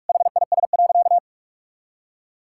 Morse code, heard as a keyed tone, about 46 wpm, 700 Hz. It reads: HIS.
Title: Sentences from Top 1000 Words - Review Part 6 - 45wpm